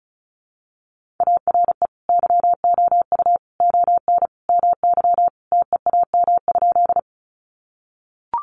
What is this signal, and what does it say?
Morse code, heard as a keyed tone, about 34 wpm, 700 Hz. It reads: ARE YOU ON MY TEAM?
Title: Sentences from Top 1500 Words 35wpm